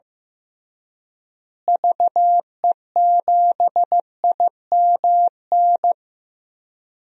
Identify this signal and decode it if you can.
Morse code, heard as a keyed tone, about 15 wpm, 700 Hz. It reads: VE7IMN
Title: DXCC-100 Call Signs - 6 Characters - Repeated 2x - 15wpm